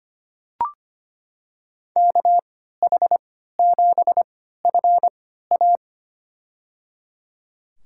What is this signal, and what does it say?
Morse code, heard as a keyed tone, about 25 wpm, spaced as for 14 wpm, 700 Hz. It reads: KH7FA
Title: USA Call Signs - 5 Characters - 3x Character Spacing - 25wpm